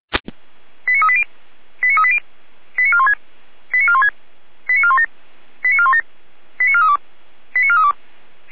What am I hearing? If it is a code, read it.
Five-tone selective call (ZVEI), heard as quick runs of five tones, tones of about 70 ms. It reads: 89380 89380 89416 89416 89417 89417 89432 89432
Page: /455/SDR/ZVEI/89108/